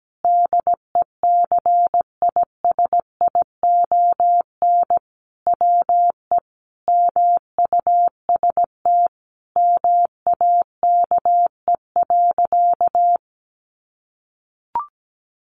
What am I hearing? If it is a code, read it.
Morse code, heard as a keyed tone, about 17 wpm, 700 Hz. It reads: DECISION WE MUST MAKE.
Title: Sentences from Top 1000 Words 17wpm